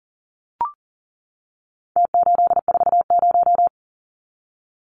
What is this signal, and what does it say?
Morse code, heard as a keyed tone, about 40 wpm, 700 Hz. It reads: T840
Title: Numbers - Contest Style - 4-Digit 40wpm